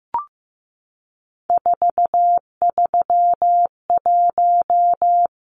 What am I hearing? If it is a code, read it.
Morse code, heard as a keyed tone, about 15 wpm, 700 Hz. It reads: 431
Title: Numbers - 3-Digit 15wpm